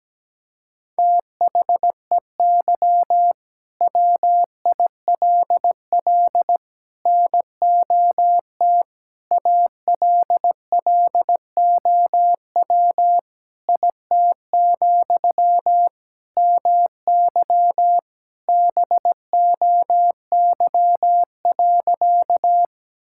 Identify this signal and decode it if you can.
Morse code, heard as a keyed tone, about 17 wpm, 700 Hz. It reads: THEY WILL NOT ALLOW IT, MY BOY.
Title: Sentences from Top 1000 Words - Review Part 5 - 17wpm